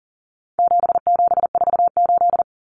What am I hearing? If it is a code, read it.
Morse code, heard as a keyed tone, about 40 wpm, 700 Hz. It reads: 7748
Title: Common Cut Numbers - 4-Digit 40wpm